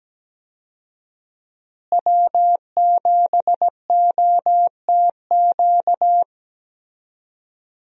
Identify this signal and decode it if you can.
Morse code, heard as a keyed tone, about 17 wpm, 700 Hz. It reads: W7OTQ
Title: USA Call Signs - 5 Characters - Repeated 2x - 17wpm